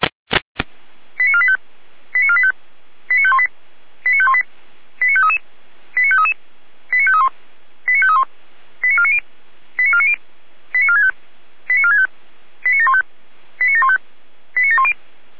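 Five-tone selective call (ZVEI), heard as quick runs of five tones, tones of about 70 ms. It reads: 89485 89485 89518 89518 89533 89533 89531 89531 89490 89490 89465 89465 89715 89715 89710
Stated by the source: Five-tone ZVEI selective call 89017, received on an SDR.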